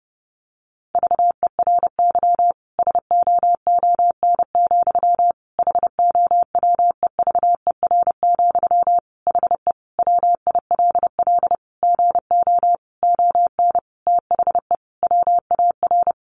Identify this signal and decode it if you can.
Morse code, heard as a keyed tone, about 30 wpm, 700 Hz. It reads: VERY SOON, HOWEVER, HE WILL GO ON THE WAR
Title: Sentences from Top 1400 Words 30wpm